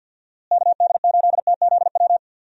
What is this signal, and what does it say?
Morse code, heard as a keyed tone, about 49 wpm, 700 Hz. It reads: KD8TZW
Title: USA Call Signs - 6 Characters - Repeated 3x - 50wpm